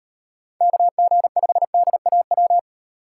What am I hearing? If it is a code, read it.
Morse code, heard as a keyed tone, about 35 wpm, 700 Hz. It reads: KG5DAW